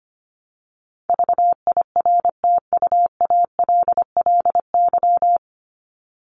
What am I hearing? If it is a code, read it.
Morse code, heard as a keyed tone, about 25 wpm, 700 Hz. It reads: VIRTUALLY
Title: LY Suffix - Top 100 Words with Base Words 25wpm